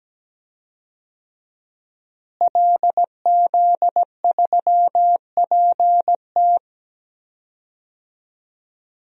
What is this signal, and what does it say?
Morse code, heard as a keyed tone, about 17 wpm, 700 Hz. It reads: LZ3PT
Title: DXCC-100 Call Signs - 5 Characters - Repeated 3x - 17wpm